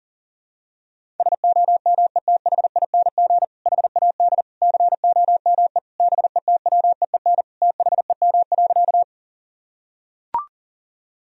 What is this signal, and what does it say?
Morse code, heard as a keyed tone, about 40 wpm, 700 Hz. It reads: SOMETHING HAD COME BETWEEN THEM.
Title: Sentences from Top 1000 Words - Review Part 1 - 40wpm